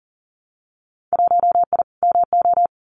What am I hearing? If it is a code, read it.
Morse code, heard as a keyed tone, about 41 wpm, 700 Hz. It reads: 1I MO
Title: Winter Field Day Exchanges Encore 40wpm